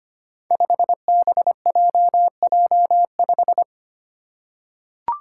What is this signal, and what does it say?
Morse code, heard as a keyed tone, about 25 wpm, 700 Hz. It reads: 5BJJ5
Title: Five Challenging Letter-Number Combinations 25wpm